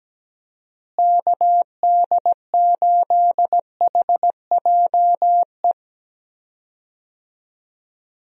Morse code, heard as a keyed tone, about 17 wpm, 700 Hz. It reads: KD8HJE
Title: USA Call Signs - 6 Characters - Repeated 2x - 17wpm